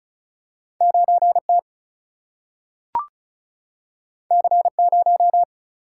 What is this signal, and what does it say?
Morse code, heard as a keyed tone, about 36 wpm, 700 Hz. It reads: 9T C0
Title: Single-Letter Single-Number 35wpm